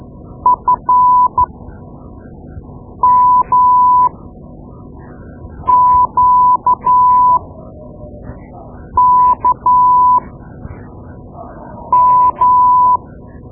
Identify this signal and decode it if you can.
Morse code, heard as a keyed tone, about 13 wpm, spaced as for 5 wpm, 1000 Hz. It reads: FMQKM
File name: Morse Lessons / Morse Lessons / Tape-2B